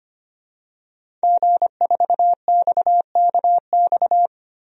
Morse code, heard as a keyed tone, about 25 wpm, 700 Hz. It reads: G4XKX
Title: DXCC-100 Call Signs - 5 Characters - 25wpm